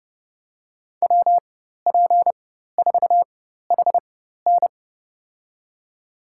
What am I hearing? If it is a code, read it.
Morse code, heard as a keyed tone, about 30 wpm, spaced as for 14 wpm, 700 Hz. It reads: WP4HN
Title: USA Call Signs - 5 Characters - 4x Character Spacing - 30wpm